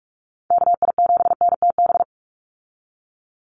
Morse code, heard as a keyed tone, about 44 wpm, 700 Hz. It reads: KI7NTB
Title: USA Call Signs 45wpm